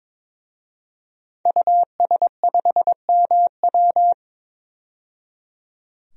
Morse code, heard as a keyed tone, about 22 wpm, 700 Hz. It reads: US5MW
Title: DXCC-100 Call Signs - 5 Characters - 22wpm